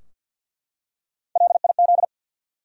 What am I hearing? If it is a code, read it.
Morse code, heard as a keyed tone, about 51 wpm, 700 Hz. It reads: LIZ
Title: USA Common Names - Max 4 Letters - 50wpm